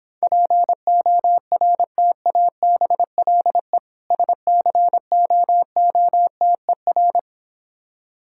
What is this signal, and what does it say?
Morse code, heard as a keyed tone, about 26 wpm, 700 Hz. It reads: PORTABLE SCOOTER